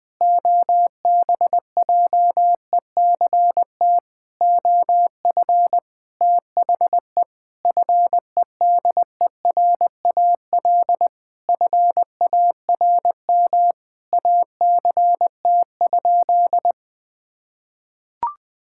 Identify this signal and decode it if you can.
Morse code, heard as a keyed tone, about 20 wpm, 700 Hz. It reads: OBJECT OF THE FEDERAL FARM ACT?
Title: Sentences from Top 1000 Words - Review Part 5 - 20wpm